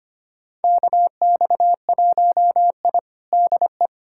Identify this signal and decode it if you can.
Morse code, heard as a keyed tone, about 25 wpm, 700 Hz. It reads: KX1I DE
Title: USA Callsign DE Callsign - 4 Characters - 25wpm